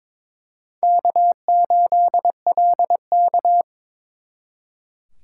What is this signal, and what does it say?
Morse code, heard as a keyed tone, about 26 wpm, 700 Hz. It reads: K8LK
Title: Speed Racing - USA Call Signs 22wpm